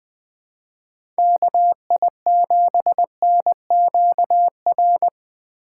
Morse code, heard as a keyed tone, about 20 wpm, 700 Hz. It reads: KI7NQR
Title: USA Call Signs - 6 Characters - 2x Character Spacing - 20wpm